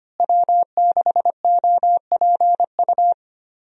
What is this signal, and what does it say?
Morse code, heard as a keyed tone, about 25 wpm, 700 Hz. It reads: W6OPU
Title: USA Call Signs - 5 Characters - Repeated 3x - 25wpm